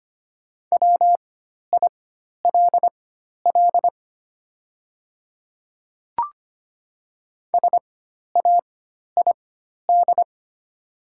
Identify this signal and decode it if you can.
Morse code, heard as a keyed tone, about 25 wpm, spaced as for 12 wpm, 700 Hz. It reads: WILL SAID
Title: 4x Character Spacing - Top 100 Words - 25wpm